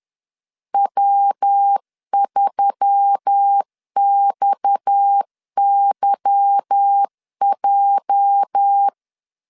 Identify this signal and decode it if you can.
Morse code, heard as a keyed tone, about 10 wpm, 800 Hz. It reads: W3XYJ
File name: LEZIONI / 9 286 10wpm 2735